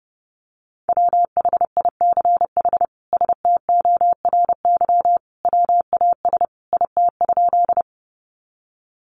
Morse code, heard as a keyed tone, about 28 wpm, 700 Hz. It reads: WHICH STORY WAS IT?
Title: Sentences from Top 300 Words 30wpm